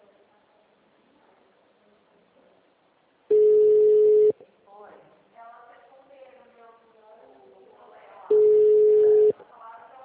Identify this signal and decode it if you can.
Ringback tone (Europe, 425 Hz), with speech.